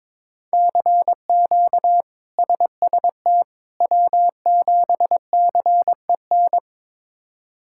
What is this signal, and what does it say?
Morse code, heard as a keyed tone, about 22 wpm, 700 Hz. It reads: CQ SST W7CEN
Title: K1USN SST QSO Practice - 22wpm